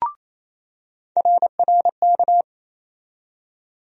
Morse code, heard as a keyed tone, about 28 wpm, 700 Hz. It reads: RRK
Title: USA Call Sign Postfixes - 3 Letters 28wpm